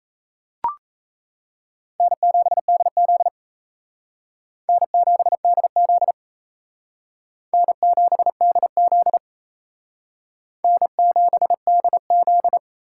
Morse code, heard as a keyed tone, about 34 wpm, 700 Hz. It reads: N7DZ N7DZ N7DZ N7DZ
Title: Speed Racing - USA 1x2 Call Signs 28wpm